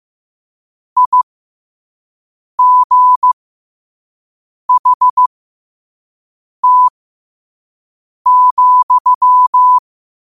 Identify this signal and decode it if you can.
Morse code, heard as a keyed tone, about 15 wpm, spaced as for 5 wpm, 1000 Hz. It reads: IGHT,
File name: qotdfiles / 5wpm